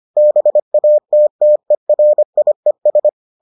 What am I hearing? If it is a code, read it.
Morse code, heard as a keyed tone, about 25 wpm, 600 Hz. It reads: BATTERIES